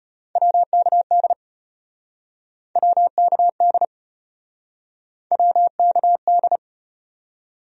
Morse code, heard as a keyed tone, about 36 wpm, 700 Hz. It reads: WKD WKD WKD